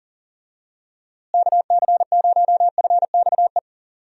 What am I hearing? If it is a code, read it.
Morse code, heard as a keyed tone, about 40 wpm, 700 Hz. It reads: KC0FXE
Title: USA Call Signs - 6 Characters - Repeated 2x - 40wpm